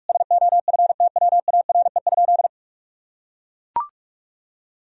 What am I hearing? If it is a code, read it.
Morse code, heard as a keyed tone, about 41 wpm, 700 Hz. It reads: SOFTWARE?
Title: Sentences from Top 1000 Words - Review Part 3 - 45wpm